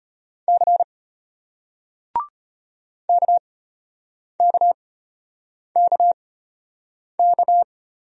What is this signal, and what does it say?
Morse code, heard as a keyed tone, about 33 wpm, spaced as for 8 wpm, 700 Hz. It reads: C KKKK